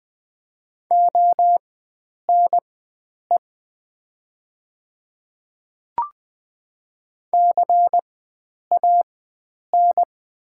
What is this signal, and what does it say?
Morse code, heard as a keyed tone, about 20 wpm, spaced as for 9 wpm, 700 Hz. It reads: ONE CAN
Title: 4x Character Spacing - Top 100 Words - 20wpm